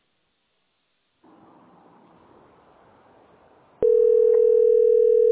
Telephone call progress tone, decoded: ringback tone

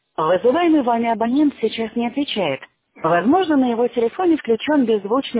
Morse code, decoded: none

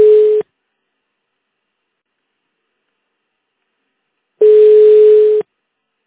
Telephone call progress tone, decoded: ringback tone

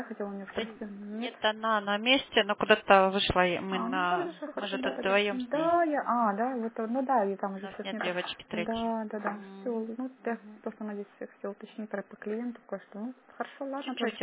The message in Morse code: none

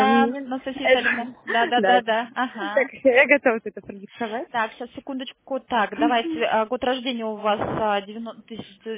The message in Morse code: none

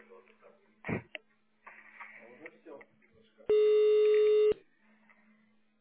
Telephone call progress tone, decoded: ringback tone

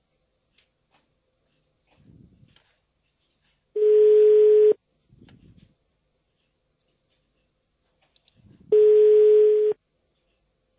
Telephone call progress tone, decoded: ringback tone

